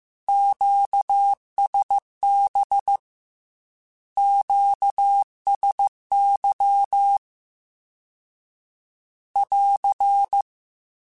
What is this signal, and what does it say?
Morse code, heard as a keyed tone, about 15 wpm, 800 Hz. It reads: QSB QSY <AR>
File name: Q Calls / Q Calls-mp3s / Q Calls Once @13